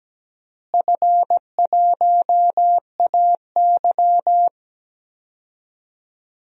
Morse code, heard as a keyed tone, about 17 wpm, 700 Hz. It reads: F1AY
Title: DXCC-100 Call Signs - 4 Characters - Repeated 3x - 17wpm